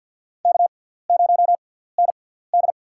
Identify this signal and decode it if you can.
Morse code, heard as a keyed tone, about 53 wpm, spaced as for 17 wpm, 700 Hz. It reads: K0ND
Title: USA Call Signs - 4 Characters - 6x Character Spacing - 50wpm